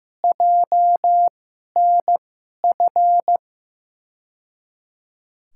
Morse code, heard as a keyed tone, about 15 wpm, spaced as for 11 wpm, 700 Hz. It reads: JNF
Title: USA Call Signs - 2x Character Spacing - 15wpm